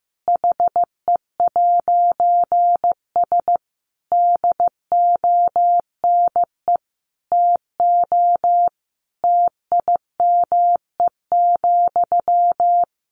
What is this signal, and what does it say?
Morse code, heard as a keyed tone, about 15 wpm, 700 Hz. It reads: HE'S DONE TO TIME,
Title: A Sound Of Thunder 15wpm